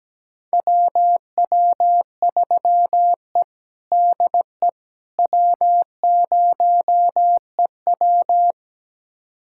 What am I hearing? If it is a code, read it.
Morse code, heard as a keyed tone, about 17 wpm, 700 Hz. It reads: WW3E DE W0EW
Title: USA Callsign DE Callsign - 4 Characters - 17wpm